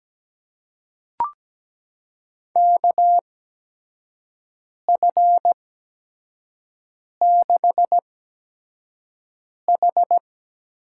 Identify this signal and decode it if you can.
Morse code, heard as a keyed tone, about 17 wpm, spaced as for 5 wpm, 700 Hz. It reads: KF6H
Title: USA Call Signs - 8x Character Spacing - 17wpm